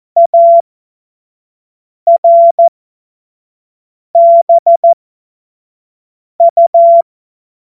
Morse code, heard as a keyed tone, about 14 wpm, spaced as for 5 wpm, 680 Hz. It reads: ARBU
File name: 25 F70